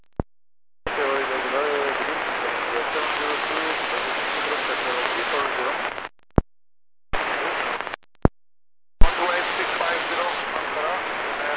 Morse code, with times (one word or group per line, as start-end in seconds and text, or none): none